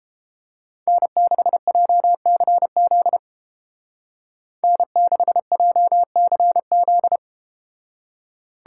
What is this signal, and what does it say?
Morse code, heard as a keyed tone, about 31 wpm, 700 Hz. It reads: N6JCZ N6JCZ